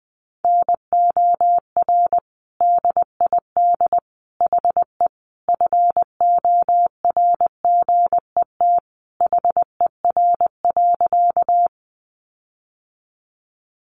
Morse code, heard as a keyed tone, about 20 wpm, 700 Hz. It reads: NOR DID HE FORGET HER.
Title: Sentences from Top 1500 Words 20wpm